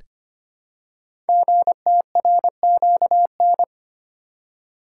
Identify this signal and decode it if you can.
Morse code, heard as a keyed tone, about 25 wpm, 700 Hz. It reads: GTRQN
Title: Code Groups - 5 Letters - 25wpm